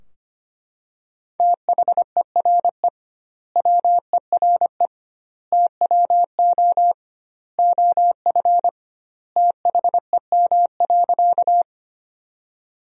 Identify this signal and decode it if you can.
Morse code, heard as a keyed tone, about 25 wpm, 700 Hz. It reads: THERE WERE TWO OF THEM.
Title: 2x Word-Spacing - Sentences from Top 100 Words - 25wpm